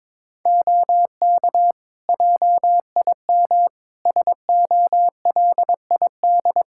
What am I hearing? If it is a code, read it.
Morse code, heard as a keyed tone, about 22 wpm, 700 Hz. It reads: OK JIM SOLID
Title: Practice QSO - Exchange 3 - More Info - 22wpm